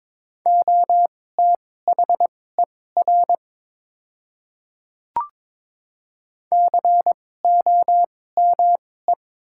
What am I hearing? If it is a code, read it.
Morse code, heard as a keyed tone, about 22 wpm, spaced as for 16 wpm, 700 Hz. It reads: OTHER COME
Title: 2x Character Spacing - Top 100 Words - 22wpm